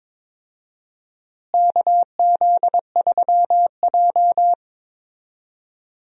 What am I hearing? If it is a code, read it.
Morse code, heard as a keyed tone, about 21 wpm, 700 Hz. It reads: KZ3J
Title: USA Call Signs - 4 Characters - Repeated 3x - 22wpm